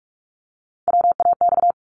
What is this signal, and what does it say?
Morse code, heard as a keyed tone, about 46 wpm, 700 Hz. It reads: WAX
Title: Multi-letter Phonemes 45wpm